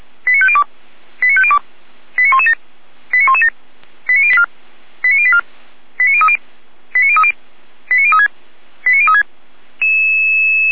Five-tone selective call (ZVEI), heard as quick runs of five tones, tones of about 70 ms. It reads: 89502 89502 89117 89117 89084 89030 89030 89036 89036